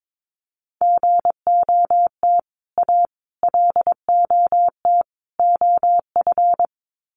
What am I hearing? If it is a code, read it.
Morse code, heard as a keyed tone, about 22 wpm, 700 Hz. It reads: GOT A LOT OF
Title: Sets of 4 Words 22wpm